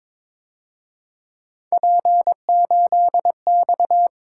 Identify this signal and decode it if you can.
Morse code, heard as a keyed tone, about 27 wpm, 700 Hz. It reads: P8X
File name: Speed-Racing - Three Letter-Number Combinations - 22wpm